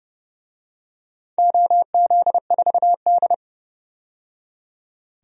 Morse code, heard as a keyed tone, about 31 wpm, 700 Hz. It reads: OZ4D
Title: DXCC-100 Call Signs - 4 Characters - 30wpm